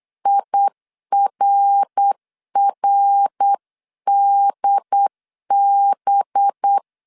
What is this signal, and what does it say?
Morse code, heard as a keyed tone, about 8 wpm, 800 Hz. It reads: IRRDB
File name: CW / DET / 1 6 1500C S B 8wpm 5142